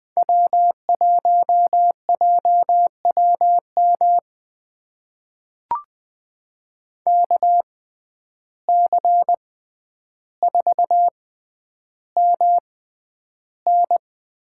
Morse code, heard as a keyed tone, about 21 wpm, spaced as for 10 wpm, 700 Hz. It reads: W1JWM KC4MN
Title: USA Call Signs - 5 Characters - 6x Character Spacing - 20wpm